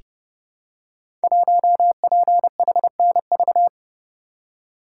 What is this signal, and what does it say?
Morse code, heard as a keyed tone, about 30 wpm, 700 Hz. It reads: 1PHNV
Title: Code Groups - 5 Letter:Numbers - 30wpm